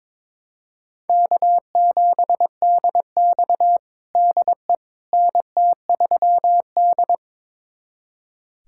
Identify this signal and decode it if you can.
Morse code, heard as a keyed tone, about 22 wpm, 700 Hz. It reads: K7DX DE NT3D
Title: USA Callsign DE Callsign - 4 Characters - 22wpm